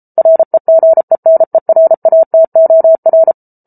Morse code, heard as a keyed tone, about 34 wpm, 650 Hz. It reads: REGENERATOR